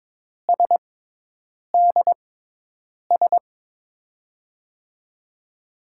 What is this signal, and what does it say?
Morse code, heard as a keyed tone, about 22 wpm, spaced as for 8 wpm, 700 Hz. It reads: SDS
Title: USA Call Signs - 5 Characters - 6x Character Spacing - 22wpm